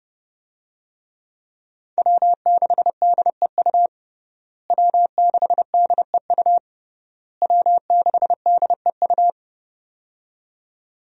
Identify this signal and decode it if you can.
Morse code, heard as a keyed tone, about 30 wpm, 700 Hz. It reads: W6DEU W6DEU W6DEU
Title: USA Call Signs - 5 Characters - Repeated 3x - 30wpm